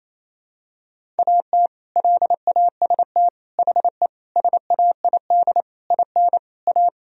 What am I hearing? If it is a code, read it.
Morse code, heard as a keyed tone, about 28 wpm, 700 Hz. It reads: AT LAST HE SAID IN A
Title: Sentences from Top 1000 Words - Review Part 1 - 28wpm